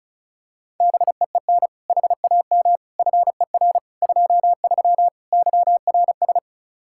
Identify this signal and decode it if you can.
Morse code, heard as a keyed tone, about 36 wpm, 700 Hz. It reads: BEEN HAM FER 23 YRS
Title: Practice QSO - Exchange 3 - More Info - 35wpm